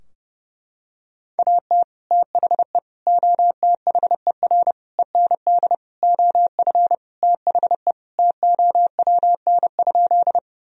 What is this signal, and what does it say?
Morse code, heard as a keyed tone, about 28 wpm, 700 Hz. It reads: AT THE OTHER END OF THE TOWN?
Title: Sentences from Top 1000 Words - Review Part 2 - 30wpm